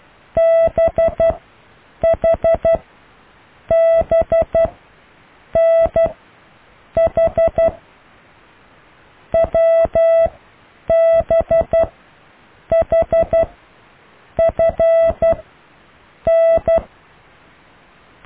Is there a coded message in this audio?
Yes: Morse code from 0.4 to 16.8 s, heard as a keyed tone, about 12 wpm, spaced as for 7 wpm, 660 Hz. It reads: BHBNH WBHFN